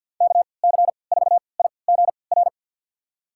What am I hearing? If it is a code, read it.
Morse code, heard as a keyed tone, about 51 wpm, spaced as for 28 wpm, 700 Hz. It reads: KC4IGR
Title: USA Call Signs - 6 Characters - 3x Character Spacing - 50wpm